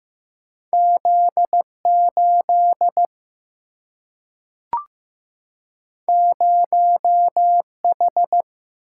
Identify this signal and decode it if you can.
Morse code, heard as a keyed tone, about 15 wpm, 700 Hz. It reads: Z8 0H
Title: Single-Letter Single-Number 15wpm